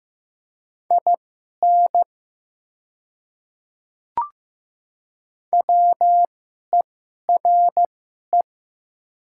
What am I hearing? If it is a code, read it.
Morse code, heard as a keyed tone, about 15 wpm, spaced as for 11 wpm, 700 Hz. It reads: IN WERE
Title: 2x Character Spacing - Top 200 Words - 15wpm